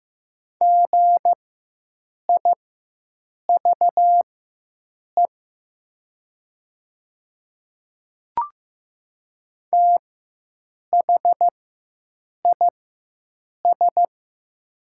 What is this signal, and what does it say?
Morse code, heard as a keyed tone, about 15 wpm, spaced as for 7 wpm, 700 Hz. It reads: GIVE THIS
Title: 4x Character Spacing - Top 200 Words - 15wpm